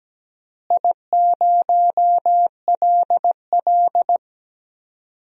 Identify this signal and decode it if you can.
Morse code, heard as a keyed tone, about 17 wpm, 700 Hz. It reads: I0LL
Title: DXCC-100 Call Signs - 4 Characters - Repeated 2x - 17wpm